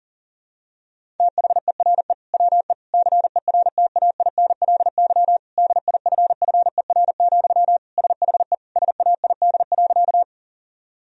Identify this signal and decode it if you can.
Morse code, heard as a keyed tone, about 37 wpm, 700 Hz. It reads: THERE WE CERTAINLY DIFFER, SHE SAID.
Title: Sentences from Top 1000 Words - Review Part 1 - 40wpm